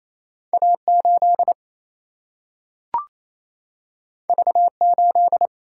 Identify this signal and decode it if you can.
Morse code, heard as a keyed tone, about 28 wpm, 700 Hz. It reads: A8 V8
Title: Single-Letter Single-Number 28wpm